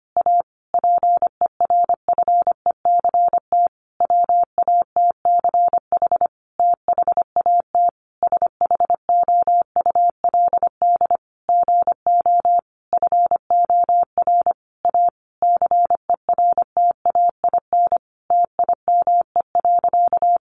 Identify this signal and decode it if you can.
Morse code, heard as a keyed tone, about 25 wpm, 700 Hz. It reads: A PERFECT WATCH THAT SHOULD GO FOR A CERTAIN TIME.
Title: Sentences from Top 1000 Words 25wpm